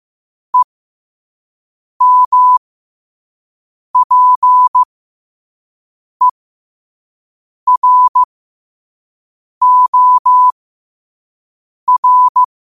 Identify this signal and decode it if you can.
Morse code, heard as a keyed tone, about 15 wpm, spaced as for 5 wpm, 1000 Hz. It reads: EMPEROR
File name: qotdfiles / 5wpm